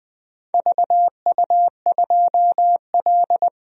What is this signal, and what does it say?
Morse code, heard as a keyed tone, about 20 wpm, 700 Hz. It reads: VU2L
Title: DXCC-100 Call Signs - 4 Characters - 20wpm